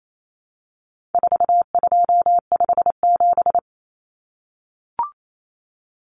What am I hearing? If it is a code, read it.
Morse code, heard as a keyed tone, about 28 wpm, 700 Hz. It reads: 4257
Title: Numbers - 4-Digit 28wpm